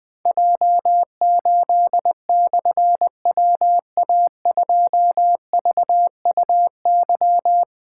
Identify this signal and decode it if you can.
Morse code, heard as a keyed tone, about 20 wpm, 700 Hz. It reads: J8/WA2VUY
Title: Difficult Call Signs 20wpm